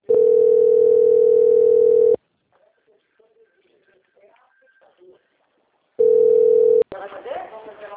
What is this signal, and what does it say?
Ringback tone (North America), 440 + 480 Hz, with speech.
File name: out- - - - - .36190